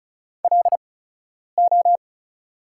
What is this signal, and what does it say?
Morse code, heard as a keyed tone, about 36 wpm, spaced as for 10 wpm, 690 Hz. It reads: LO
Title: 8x Character Spacing - Top 200 Words - 35wpm